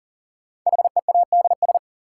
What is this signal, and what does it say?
Morse code, heard as a keyed tone, about 40 wpm, 700 Hz. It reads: HEADS
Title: Sentences from Top 1000 Words - Review Part 3 - New Word Drill - 40wpm